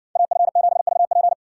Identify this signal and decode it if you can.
Morse code, heard as a keyed tone, about 60 wpm, 700 Hz. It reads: AF7VP